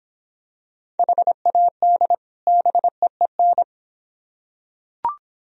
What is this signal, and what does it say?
Morse code, heard as a keyed tone, about 26 wpm, 700 Hz. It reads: HAD BEEN